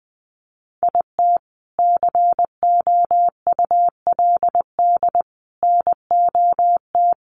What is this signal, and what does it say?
Morse code, heard as a keyed tone, about 20 wpm, 700 Hz. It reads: IT COULD NOT
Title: Sentences from Top 1200 Words 20wpm